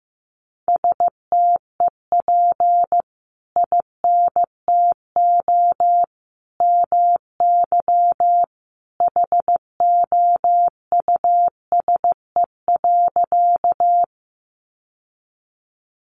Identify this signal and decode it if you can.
Morse code, heard as a keyed tone, about 15 wpm, 700 Hz. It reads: STEP INTO MY HOUSE.
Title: Sentences from Top 600 Words 15wpm